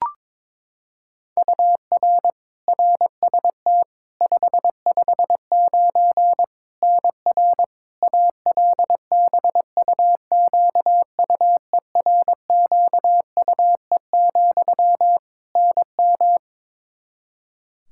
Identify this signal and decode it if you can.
Morse code, heard as a keyed tone, about 22 wpm, 700 Hz. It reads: UR RST 559 NR ALBUQUERQUE, NM
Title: Practice QSO - Exchange 1 - Beginning - 22wpm